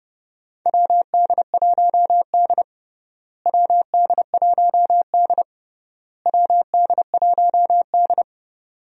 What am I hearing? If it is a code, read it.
Morse code, heard as a keyed tone, about 30 wpm, 700 Hz. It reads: WD1D WD1D WD1D
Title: USA Call Signs - 4 Characters - Repeated 3x - 30wpm